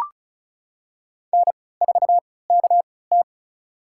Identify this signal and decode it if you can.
Morse code, heard as a keyed tone, about 36 wpm, spaced as for 20 wpm, 700 Hz. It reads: N4KT
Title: USA Call Signs - 3x Character Spacing - 35wpm